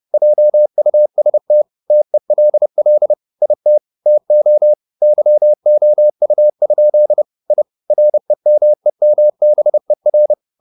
Morse code, heard as a keyed tone, about 30 wpm, 600 Hz. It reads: JUST TELL IT TO YOU? I REMEMBER
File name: the old man and the sea / the old man and the sea 1 30wpm